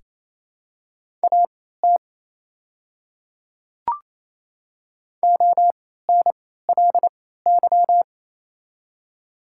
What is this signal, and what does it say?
Morse code, heard as a keyed tone, about 28 wpm, spaced as for 16 wpm, 700 Hz. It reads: AT ONLY
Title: 3x Character Spacing - Top 200 Words - 28wpm